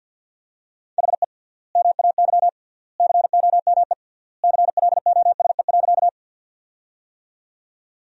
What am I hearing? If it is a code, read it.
Morse code, heard as a keyed tone, about 50 wpm, 700 Hz. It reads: HE MAY COME CLOSE.
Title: 3x Word-Spacing - Sentences from Top 300 Words - 50wpm